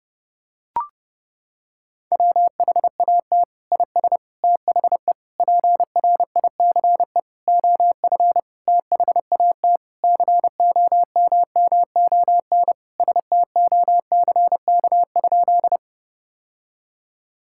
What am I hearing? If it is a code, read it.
Morse code, heard as a keyed tone, about 30 wpm, 700 Hz. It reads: WHAT IS THE PRICE OF THAT COMMON STOCK?
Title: Sentences from Top 1000 Words - Review Part 5 - 30wpm